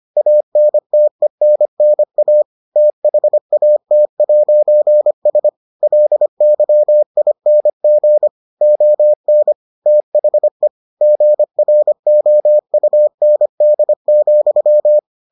Morse code, heard as a keyed tone, about 25 wpm, 600 Hz. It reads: ANTENNA THAT'S LYING ON THE GROUND,